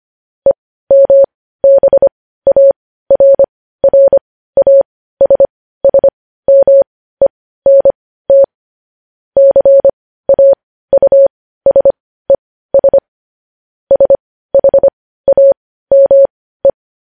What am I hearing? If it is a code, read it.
Morse code, heard as a keyed tone, about 25 wpm, spaced as for 15 wpm, 550 Hz. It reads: EMBARRASSMENT CAUSES SHAME
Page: news-headlines-in-morse-code-at-15-wpm